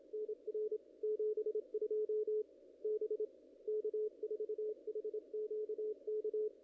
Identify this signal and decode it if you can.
Morse code, heard as a keyed tone, about 26 wpm, 440 Hz. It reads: NR 72 B K4HQK
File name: contests / ss-cw-k5tr-op-n5rz -0000